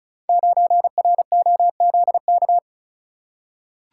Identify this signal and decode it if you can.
Morse code, heard as a keyed tone, about 36 wpm, 700 Hz. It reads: 9ROZK
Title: Code Groups - 5 Letter:Numbers - 35wpm